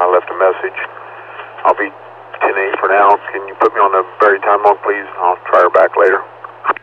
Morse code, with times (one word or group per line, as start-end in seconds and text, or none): none